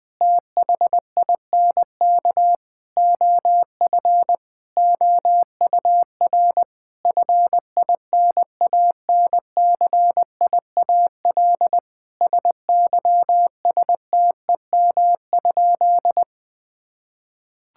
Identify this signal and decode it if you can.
Morse code, heard as a keyed tone, about 21 wpm, 700 Hz. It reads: THINK OF OUR FINANCIAL SYSTEM?
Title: Sentences from Top 1000 Words - Review Part 5 - 20wpm